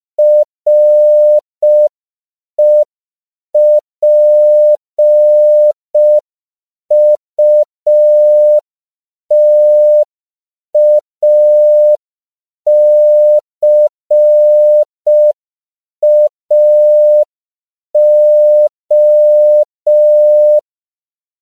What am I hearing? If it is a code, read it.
Morse code, heard as a keyed tone, about 5 wpm, 600 Hz. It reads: REPUTACAO